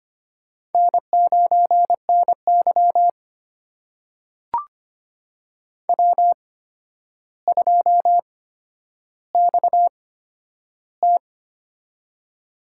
Morse code, heard as a keyed tone, about 25 wpm, spaced as for 11 wpm, 700 Hz. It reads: N9NY W2XT
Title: USA Call Signs - 4 Characters - 8x Character Spacing - 25wpm